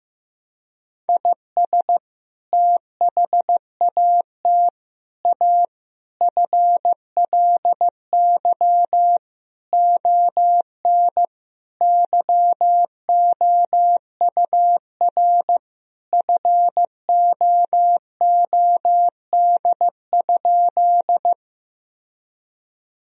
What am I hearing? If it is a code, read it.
Morse code, heard as a keyed tone, about 15 wpm, 700 Hz. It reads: IS THAT A FLY ON YOUR FOOD?
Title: Sentences from Top 1000 Words - Review Part 2 - 15wpm